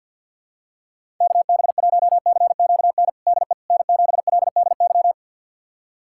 Morse code, heard as a keyed tone, about 51 wpm, 700 Hz. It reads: KB1CQN DE N7LDY
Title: USA Callsign DE Callsign 50wpm